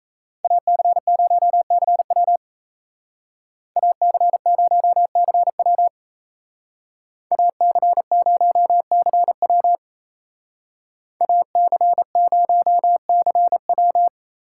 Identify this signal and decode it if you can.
Morse code, heard as a keyed tone, about 33 wpm, 700 Hz. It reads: AC0CW AC0CW AC0CW AC0CW
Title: Speed Racing - USA 2x2 Call Signs 28wpm